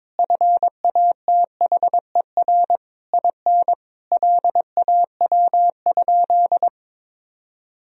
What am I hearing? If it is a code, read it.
Morse code, heard as a keyed tone, about 22 wpm, 700 Hz. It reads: FATHER IN LAW?
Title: Sentences from Top 1000 Words - Review Part 4 - 22wpm